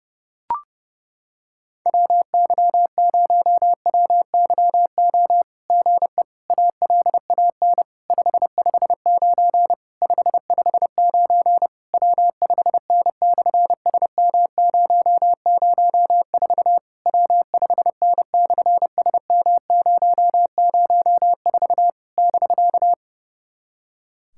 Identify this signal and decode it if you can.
Morse code, heard as a keyed tone, about 30 wpm, 700 Hz. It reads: WY0WYO GE ALAN 559 559 W5N/SM004 W5N/SM004 <BK>
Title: SOTA QSO Practice 30wpm